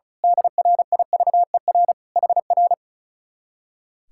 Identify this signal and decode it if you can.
Morse code, heard as a keyed tone, about 33 wpm, 700 Hz. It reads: DRIVER HR
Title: Speed-Racing Common QSO Phrases Encore - 35wpm